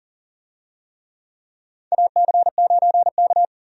Morse code, heard as a keyed tone, about 39 wpm, 700 Hz. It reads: AC9K
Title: USA Call Signs - 4 Characters - 40wpm